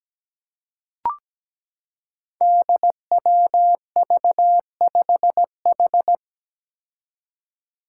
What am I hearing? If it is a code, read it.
Morse code, heard as a keyed tone, about 17 wpm, 700 Hz. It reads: DWV5H
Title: Five Challenging Letter-Number Combinations 17wpm